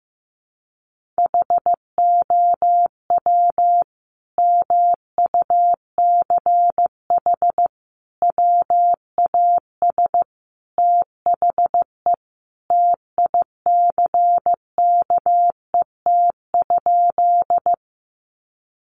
Morse code, heard as a keyed tone, about 15 wpm, 700 Hz. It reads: HOW MUCH WAS THE TICKET?
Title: Sentences from Top 1400 Words 15wpm